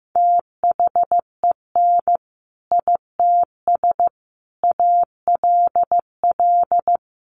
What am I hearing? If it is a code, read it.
Morse code, heard as a keyed tone, about 15 wpm, 700 Hz. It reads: THEN ITS ALL
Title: Sentences from Top 600 Words 15wpm